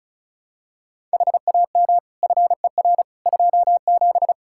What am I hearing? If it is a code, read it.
Morse code, heard as a keyed tone, about 34 wpm, 700 Hz. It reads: HAM FER 27
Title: Practice QSO - Exchange 3 - More Info - 35wpm